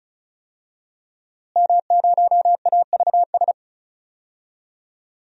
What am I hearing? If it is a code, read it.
Morse code, heard as a keyed tone, about 36 wpm, 700 Hz. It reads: M0AVS
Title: DXCC-100 Call Signs - 5 Characters - Repeated 3x - 35wpm